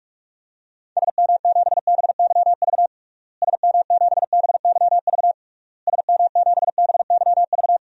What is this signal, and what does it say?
Morse code, heard as a keyed tone, about 46 wpm, 700 Hz. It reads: SM7BYV SM7BYV SM7BYV
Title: DXCC-100 Call Signs - 6 Characters - Repeated 3x - 45wpm